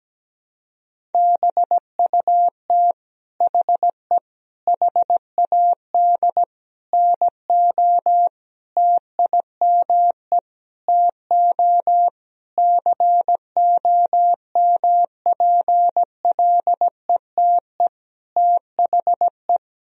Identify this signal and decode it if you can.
Morse code, heard as a keyed tone, about 17 wpm, 700 Hz. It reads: BUT HE HAD NO TIME TO COMPLETE THE
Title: Sentences from Top 1000 Words - Review Part 1 - 17wpm